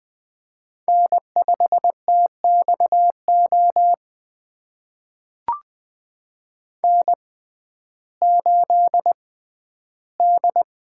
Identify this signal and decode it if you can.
Morse code, heard as a keyed tone, about 20 wpm, spaced as for 12 wpm, 700 Hz. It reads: N5TXO N8D
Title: USA Call Signs - 5 Characters - 6x Character Spacing - 20wpm